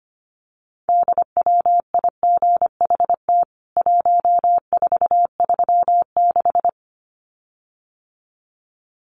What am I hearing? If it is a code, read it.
Morse code, heard as a keyed tone, about 25 wpm, 700 Hz. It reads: DWIGHT 1436
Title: CWT Exchange 25wpm